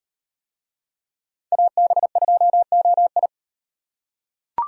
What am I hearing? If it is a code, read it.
Morse code, heard as a keyed tone, about 38 wpm, 700 Hz. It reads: AB2OI